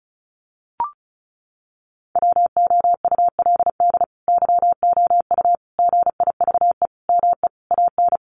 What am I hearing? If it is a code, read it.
Morse code, heard as a keyed tone, about 34 wpm, 700 Hz. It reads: WOULD YOU GIVE ME AN
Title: Sentences from Top 1000 Words 35wpm